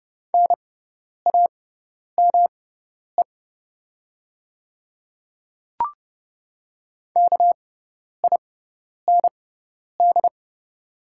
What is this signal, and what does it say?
Morse code, heard as a keyed tone, about 30 wpm, spaced as for 10 wpm, 690 Hz. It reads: NAME KIND
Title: 6x Character Spacing - Top 300 Words - 30wpm